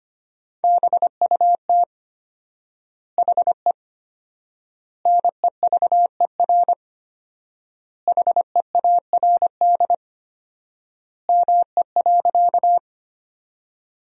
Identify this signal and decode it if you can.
Morse code, heard as a keyed tone, about 25 wpm, 700 Hz. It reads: BUT HE NEVER HEARD ME.
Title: 4x Word-Spacing - Sentences from Top 400 Words - 25wpm